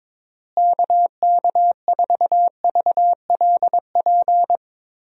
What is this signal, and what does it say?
Morse code, heard as a keyed tone, about 22 wpm, 700 Hz. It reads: KK4VLP
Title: USA Call Signs - 6 Characters - Repeated 3x - 22wpm